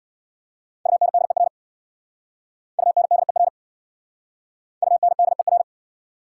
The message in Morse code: UNDER UNDER UNDER

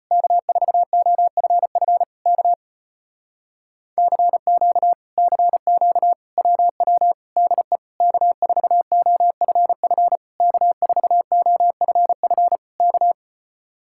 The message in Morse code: K4OFF K CQ CQ WW DE K4OFF K4OFF K

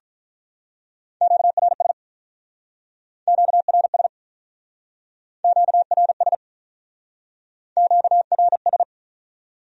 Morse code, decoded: QRS QRS QRS QRS

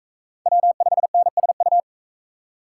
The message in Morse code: W5NSU

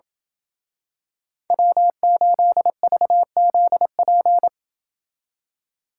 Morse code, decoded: W8VZP

